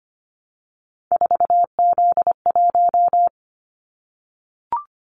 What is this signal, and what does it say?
Morse code, heard as a keyed tone, about 25 wpm, 700 Hz. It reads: 4Z1
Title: Callsign Prefixes for Top 50 DXCC - 25wpm